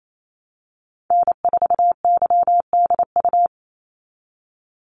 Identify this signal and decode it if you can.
Morse code, heard as a keyed tone, about 28 wpm, 700 Hz. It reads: N4YDU
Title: Call Signs - 28wpm